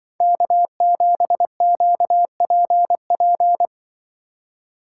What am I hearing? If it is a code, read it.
Morse code, heard as a keyed tone, about 24 wpm, 700 Hz. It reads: K7QPP
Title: Speed Racing - USA 1x3 Call Signs 20wpm